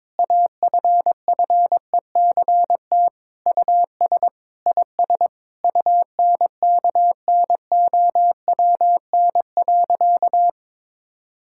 Morse code, heard as a keyed tone, about 22 wpm, 700 Hz. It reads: AFFECT US IS UNKNOWN.
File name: practice / 22wpm / Sentences with 4-Word Phrases - 22wpm